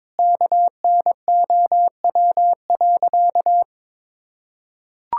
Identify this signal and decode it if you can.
Morse code, heard as a keyed tone, about 22 wpm, 700 Hz. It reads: KNOW.
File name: practice / 22wpm / Sentences with 4-Word Phrases - 22wpm